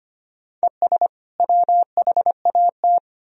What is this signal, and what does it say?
Morse code, heard as a keyed tone, about 25 wpm, 700 Hz. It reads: ES WHAT
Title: Sentences from Top 100 Words and Abbreviations 25wpm